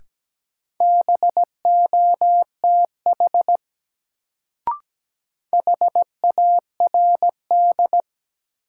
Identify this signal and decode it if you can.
Morse code, heard as a keyed tone, about 17 wpm, 700 Hz. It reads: BOTH HARD
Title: Sentences from Top 300 Words - New Word Drill - 17wpm